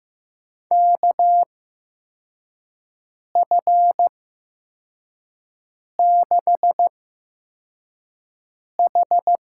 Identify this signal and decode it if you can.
Morse code, heard as a keyed tone, about 15 wpm, spaced as for 4 wpm, 700 Hz. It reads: KF6H
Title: USA Call Signs - 6 Characters - 8x Character Spacing - 15wpm